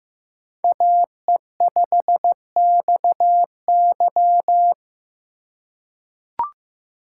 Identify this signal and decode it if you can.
Morse code, heard as a keyed tone, about 15 wpm, 700 Hz. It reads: AE5XY
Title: USA Call Signs - 5 Characters - 4x Character Spacing - 15wpm